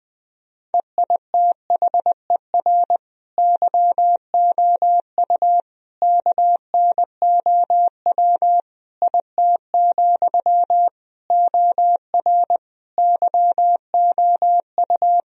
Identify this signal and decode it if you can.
Morse code, heard as a keyed tone, about 20 wpm, 700 Hz. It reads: EITHER YOU KNOW IT, OR YOU
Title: Sentences from Top 1000 Words - Review Part 4 - 20wpm